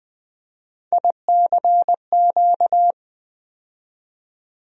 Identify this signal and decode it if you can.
Morse code, heard as a keyed tone, about 20 wpm, 700 Hz. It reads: ICQ
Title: USA Call Sign Postfixes - 3 Letters 20wpm